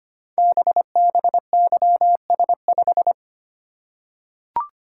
Morse code, heard as a keyed tone, about 25 wpm, 700 Hz. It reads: BBYS5
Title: Code Groups - 5 Letter:Numbers - 25wpm